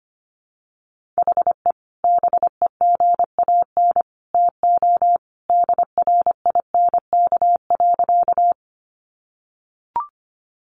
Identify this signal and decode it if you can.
Morse code, heard as a keyed tone, about 25 wpm, 700 Hz. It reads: HE BEGAN TO DRINK.
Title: Sentences from Top 1500 Words 25wpm